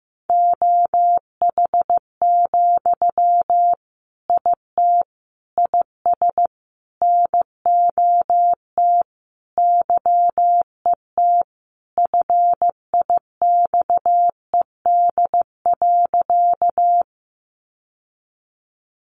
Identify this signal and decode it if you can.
Morse code, heard as a keyed tone, about 15 wpm, 700 Hz. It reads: OH, IT IS NOT YET FIXED.
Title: Sentences from Top 1100 Words 15wpm